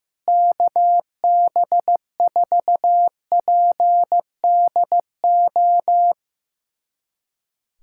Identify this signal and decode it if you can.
Morse code, heard as a keyed tone, about 15 wpm, 700 Hz. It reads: KB4PDO
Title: USA Callsign DE Callsign - 6 Characters - 15wpm